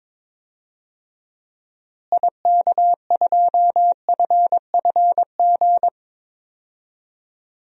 Morse code, heard as a keyed tone, about 22 wpm, 700 Hz. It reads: IK2FFG
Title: DXCC-100 Call Signs - 6 Characters - Repeated 2x - 22wpm